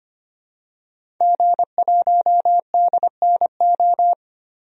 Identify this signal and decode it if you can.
Morse code, heard as a keyed tone, about 25 wpm, 700 Hz. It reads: G1DNO
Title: DXCC-100 Call Signs - 5 Characters - 25wpm